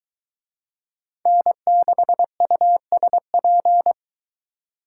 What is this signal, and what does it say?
Morse code, heard as a keyed tone, about 22 wpm, 700 Hz. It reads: N6USP